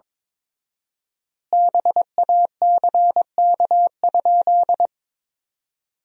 Morse code, heard as a keyed tone, about 22 wpm, 700 Hz. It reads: BACK?
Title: 4x Word-Spacing - Sentences from Top 400 Words - 22wpm